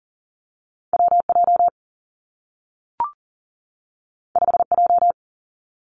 Morse code, heard as a keyed tone, about 40 wpm, 700 Hz. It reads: W J 5 J
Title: Two Challenging Letter-Number Combinations 40wpm